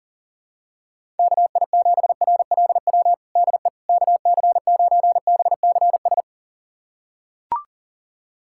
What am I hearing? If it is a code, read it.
Morse code, heard as a keyed tone, about 37 wpm, 700 Hz. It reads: KI7RLW DE KC9BCS
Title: USA Callsign DE Callsign - 6 Characters - 40wpm